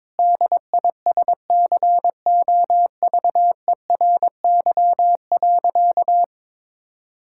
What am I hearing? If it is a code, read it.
Morse code, heard as a keyed tone, about 22 wpm, 700 Hz. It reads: DISCOVERY.
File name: practice / 22wpm / Sentences with 8-Word Phrases - 22wpm